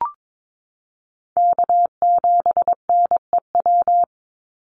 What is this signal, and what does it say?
Morse code, heard as a keyed tone, about 22 wpm, 700 Hz. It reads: K7NEW
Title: NA SOTA Chaser Callsigns 22wpm